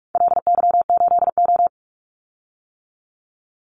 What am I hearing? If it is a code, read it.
Morse code, heard as a keyed tone, about 44 wpm, 700 Hz. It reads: LY8O
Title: Call Signs - 45wpm